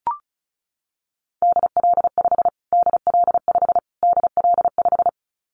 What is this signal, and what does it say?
Morse code, heard as a keyed tone, about 34 wpm, 700 Hz. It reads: DL5 DL5 DL5
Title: Callsign Prefixes for Top 25 DXCC - Repeated 3x - 35wpm